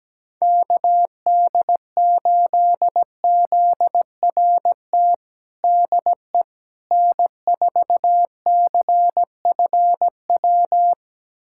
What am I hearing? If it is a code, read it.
Morse code, heard as a keyed tone, about 17 wpm, 700 Hz. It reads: KD8ZRT DE N4CFW
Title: USA Callsign DE Callsign 17wpm